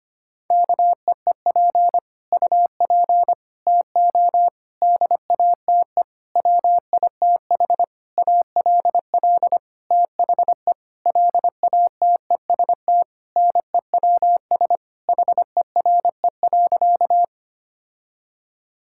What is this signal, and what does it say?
Morse code, heard as a keyed tone, about 25 wpm, 700 Hz. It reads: KEEP UP TO DATE WITH ALL THE LATEST NEWS HERE.
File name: practice / 25wpm / Sentences with 9-Word Phrases - 25wpm